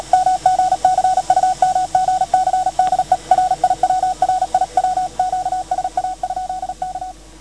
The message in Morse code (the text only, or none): MGC UMGC DE RIW RIW QSA? K